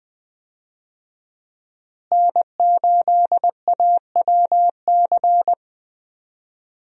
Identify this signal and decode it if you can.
Morse code, heard as a keyed tone, about 20 wpm, 700 Hz. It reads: N8AWC
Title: USA Call Signs - 5 Characters - Repeated 2x - 20wpm